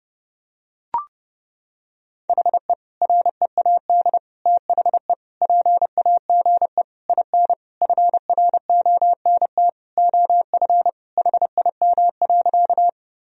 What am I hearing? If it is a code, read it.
Morse code, heard as a keyed tone, about 32 wpm, 700 Hz. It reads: HE READ THE PAGE IN FRONT OF HIM.
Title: Sentences from Top 1000 Words - Review Part 1 - 30wpm